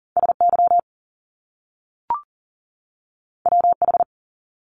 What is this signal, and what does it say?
Morse code, heard as a keyed tone, about 42 wpm, 700 Hz. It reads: SY WH
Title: Top Two Letter Combinations in Words 40wpm